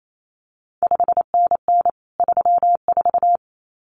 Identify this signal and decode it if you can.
Morse code, heard as a keyed tone, about 28 wpm, 700 Hz. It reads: 5NN 34
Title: CQ WW DX Contest Exchange Encore - 28wpm